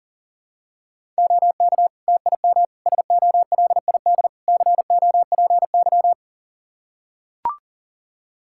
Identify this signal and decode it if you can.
Morse code, heard as a keyed tone, about 37 wpm, 700 Hz. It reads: OK TIM SOLID COPY
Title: Practice QSO - Exchange 3 - More Info - 40wpm